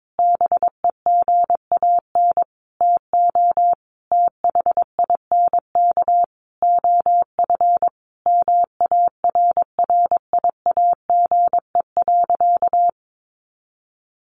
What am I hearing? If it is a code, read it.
Morse code, heard as a keyed tone, about 22 wpm, 700 Hz. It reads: BEGAN TO THINK OF MARRIAGE.
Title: Sentences from Top 800 Words 22wpm